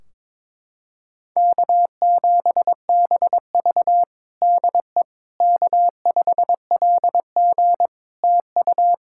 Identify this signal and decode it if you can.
Morse code, heard as a keyed tone, about 21 wpm, 700 Hz. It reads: K7BV DE K5LG TU
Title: K1USN SST QSO Practice 22wpm